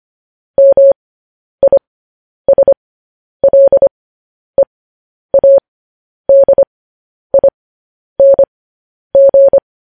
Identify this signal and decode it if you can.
Morse code, heard as a keyed tone, about 25 wpm, spaced as for 10 wpm, 550 Hz. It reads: MISLEADING
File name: podcast / News headlines-1x-25x10